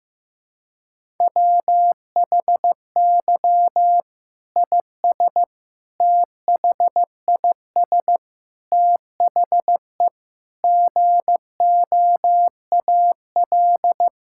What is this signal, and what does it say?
Morse code, heard as a keyed tone, about 16 wpm, 700 Hz. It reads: WHY IS THIS THE GOAL
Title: Sentences from Top 1000 Words - Review Part 5 - 15wpm